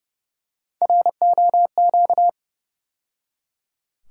Morse code, heard as a keyed tone, about 29 wpm, 700 Hz. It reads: ROQ
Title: USA Call Sign Postfixes - 3 Letters 30wpm